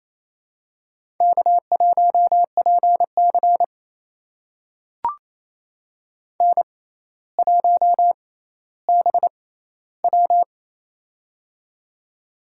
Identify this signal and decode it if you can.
Morse code, heard as a keyed tone, about 28 wpm, spaced as for 14 wpm, 700 Hz. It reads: K1PC N1BW
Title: USA Call Signs - 4 Characters - 6x Character Spacing - 28wpm